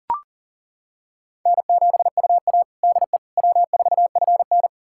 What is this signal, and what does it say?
Morse code, heard as a keyed tone, about 37 wpm, 700 Hz. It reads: N7UA DE W4FN
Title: USA Callsign DE Callsign - 4 Characters - 40wpm